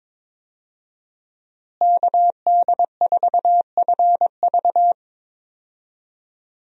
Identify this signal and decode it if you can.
Morse code, heard as a keyed tone, about 22 wpm, 700 Hz. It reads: KD4FV
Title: USA Call Signs - 5 Characters - Repeated 2x - 22wpm